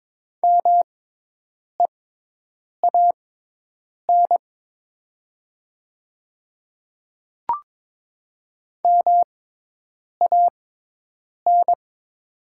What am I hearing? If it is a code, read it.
Morse code, heard as a keyed tone, about 22 wpm, spaced as for 8 wpm, 700 Hz. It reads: MEAN MAN
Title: 6x Character Spacing - Top 200 Words - 22wpm